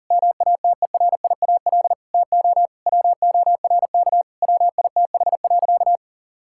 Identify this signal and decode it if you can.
Morse code, heard as a keyed tone, about 37 wpm, 700 Hz. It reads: MATERIAL TO WORK WITH.
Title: Sentences from Top 1000 Words - Review Part 5 - 40wpm